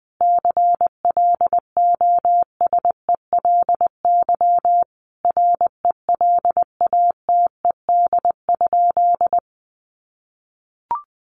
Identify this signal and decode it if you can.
Morse code, heard as a keyed tone, about 20 wpm, 700 Hz. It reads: CLOSELY RELATED?
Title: Sentences from Top 1400 Words 20wpm